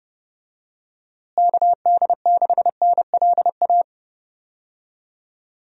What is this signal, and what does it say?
Morse code, heard as a keyed tone, about 31 wpm, 700 Hz. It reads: KD6NLA
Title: USA Call Signs - 6 Characters - Repeated 2x - 30wpm